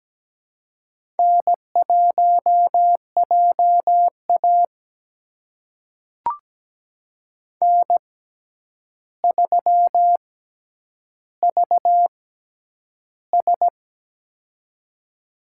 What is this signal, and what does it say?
Morse code, heard as a keyed tone, about 17 wpm, spaced as for 9 wpm, 700 Hz. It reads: N1JA N3VS
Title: USA Call Signs - 4 Characters - 6x Character Spacing - 17wpm